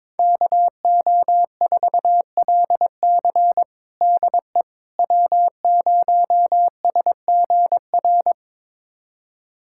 Morse code, heard as a keyed tone, about 22 wpm, 700 Hz. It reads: KO4LC DE W0SGR
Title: USA Callsign DE Callsign - 5 Characters - 22wpm